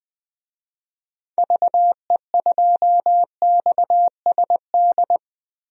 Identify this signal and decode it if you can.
Morse code, heard as a keyed tone, about 20 wpm, 700 Hz. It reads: VE2XSD
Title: DXCC-100 Call Signs - 6 Characters - 20wpm